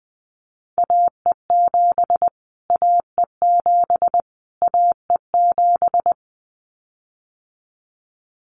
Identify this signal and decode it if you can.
Morse code, heard as a keyed tone, about 20 wpm, 700 Hz. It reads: AE7 AE7 AE7
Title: Callsign Prefixes for Top 25 DXCC - Repeated 3x - 20wpm